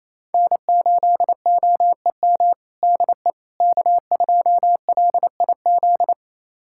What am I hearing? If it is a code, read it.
Morse code, heard as a keyed tone, about 27 wpm, 700 Hz. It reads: N8OEM DE K2LIZ
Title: USA Callsign DE Callsign 28wpm